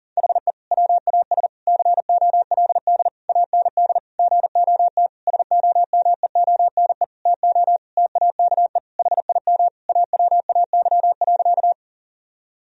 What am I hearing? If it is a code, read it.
Morse code, heard as a keyed tone, about 40 wpm, 700 Hz. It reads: HE WAS COLD AND GOT SOMEONE TO TAKE HIM AWAY.
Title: Sentences from Top 1000 Words - Review Part 4 - 40wpm